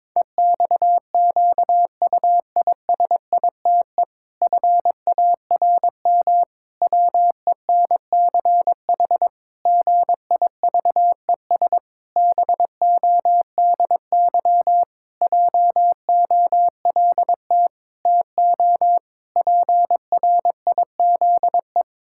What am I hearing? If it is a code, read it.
Morse code, heard as a keyed tone, about 22 wpm, 700 Hz. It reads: EXQUISITE FARM WENCH GIVES BODY JOLT TO PRIZE
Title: Pangram Challenge v2 - 22wpm